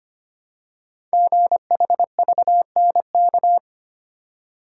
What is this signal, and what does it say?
Morse code, heard as a keyed tone, about 25 wpm, 700 Hz. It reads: GHVNK
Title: Code Groups - 5 Letters - 25wpm